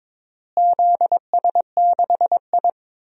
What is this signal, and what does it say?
Morse code, heard as a keyed tone, about 22 wpm, 700 Hz. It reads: ZS6I
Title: DXCC-100 Call Signs - 4 Characters - Repeated 2x - 22wpm